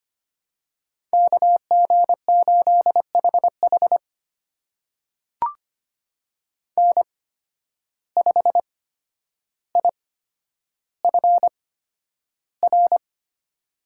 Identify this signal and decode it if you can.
Morse code, heard as a keyed tone, about 25 wpm, spaced as for 11 wpm, 700 Hz. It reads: KG8HH N5IFR
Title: USA Call Signs - 5 Characters - 8x Character Spacing - 25wpm